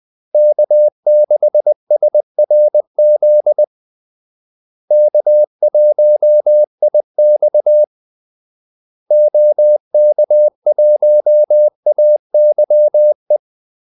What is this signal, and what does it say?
Morse code, heard as a keyed tone, about 20 wpm, 600 Hz. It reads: K6SRZ K1IX OK1AYE
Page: Kaldesignaler 20 wpm